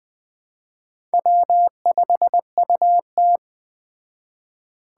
Morse code, heard as a keyed tone, about 20 wpm, 700 Hz. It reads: W5UT